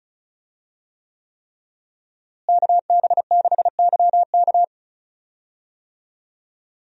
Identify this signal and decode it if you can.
Morse code, heard as a keyed tone, about 36 wpm, 700 Hz. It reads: KB6YK
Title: USA Call Signs - 5 Characters - Repeated 3x - 35wpm